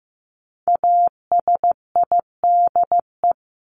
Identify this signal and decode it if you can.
Morse code, heard as a keyed tone, about 15 wpm, 700 Hz. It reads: ASIDE
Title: Top 1751-2000 Words 15wpm